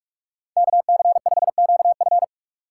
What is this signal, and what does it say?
Morse code, heard as a keyed tone, about 43 wpm, 700 Hz. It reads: KC5QF